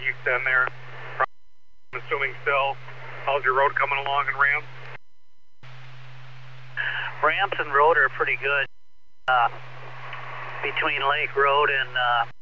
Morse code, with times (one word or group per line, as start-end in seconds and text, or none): none